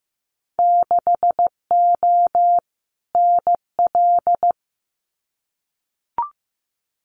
Morse code, heard as a keyed tone, about 15 wpm, 700 Hz. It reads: 6O NL
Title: Winter Field Day Exchanges Encore 15wpm